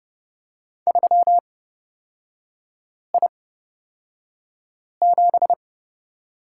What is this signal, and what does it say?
Morse code, heard as a keyed tone, about 30 wpm, spaced as for 5 wpm, 700 Hz. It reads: 3I7
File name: Lesson 36e - BT - Three Characters 30wpm